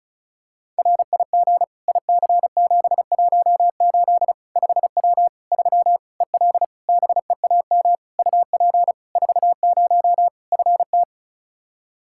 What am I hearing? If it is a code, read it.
Morse code, heard as a keyed tone, about 36 wpm, 700 Hz. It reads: RIG IC718 5W 3 EL BEAM UP 40 FT
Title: K1USN SST QSO Practice 35wpm